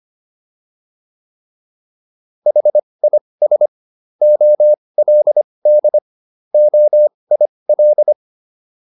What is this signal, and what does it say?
Morse code, heard as a keyed tone, about 25 wpm, spaced as for 20 wpm, 600 Hz. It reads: HIS OLD OIL